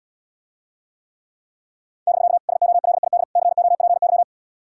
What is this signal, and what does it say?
Morse code, heard as a keyed tone, about 79 wpm, 700 Hz. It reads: <BK> SOLID COPY